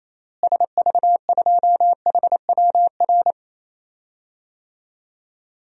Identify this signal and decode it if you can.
Morse code, heard as a keyed tone, about 28 wpm, 700 Hz. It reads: SV2HWR
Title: DXCC-100 Call Signs - 6 Characters - Repeated 2x - 28wpm